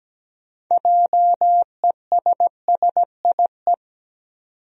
Morse code, heard as a keyed tone, about 17 wpm, 700 Hz. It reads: JESSIE
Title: USA Common Names - 17wpm